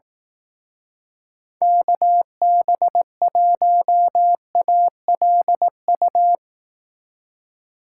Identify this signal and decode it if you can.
Morse code, heard as a keyed tone, about 19 wpm, 700 Hz. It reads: KB1ALU
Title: Speed Racing - USA 2x3 Call Signs 15wpm